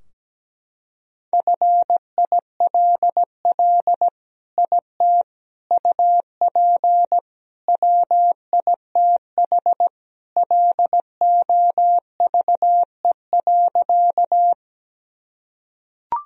Morse code, heard as a keyed tone, about 17 wpm, 700 Hz. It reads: FILL IT UP WITH LOVE.
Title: Sentences from Top 1000 Words - Review Part 3 - 17wpm